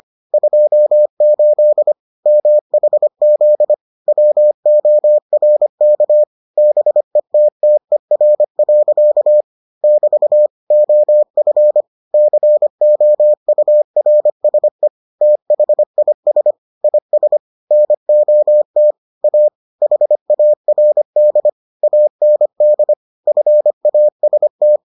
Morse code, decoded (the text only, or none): 28 MHZ WORK BETTER. <BT> OF COURSE THIS IS NOT A HARD AND FAST